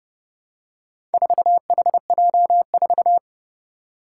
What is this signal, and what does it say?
Morse code, heard as a keyed tone, about 30 wpm, 700 Hz. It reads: 4HJ4
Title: Four Challenging Letter-Number Combinations 30wpm